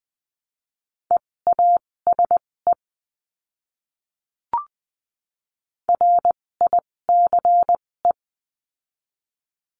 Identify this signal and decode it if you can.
Morse code, heard as a keyed tone, about 20 wpm, spaced as for 16 wpm, 700 Hz. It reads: EASE RICE
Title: 256 Lesson 42a - Words 12-20wpm